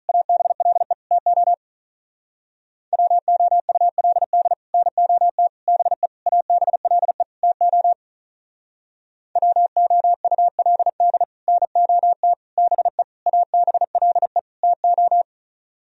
Morse code, heard as a keyed tone, about 43 wpm, 700 Hz. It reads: ABLE TO WOULD NOT BE ABLE TO WOULD NOT BE ABLE TO
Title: Speed-Racing - Sets of 5 Words 35wpm